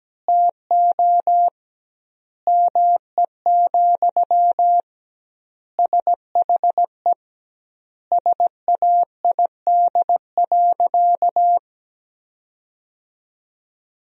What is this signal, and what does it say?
Morse code, heard as a keyed tone, about 17 wpm, 700 Hz. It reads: TO ME, SHE SAID.
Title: 2x Word-Spacing - Sentences from Top 500 Words - 17wpm